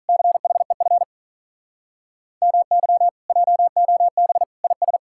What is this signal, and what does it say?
Morse code, heard as a keyed tone, about 47 wpm, 700 Hz. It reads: CHEF MY JOB IS